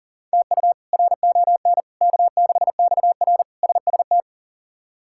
Morse code, heard as a keyed tone, about 40 wpm, 700 Hz. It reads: TU RON K6XR SST
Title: K1USN SST QSO Practice - 40wpm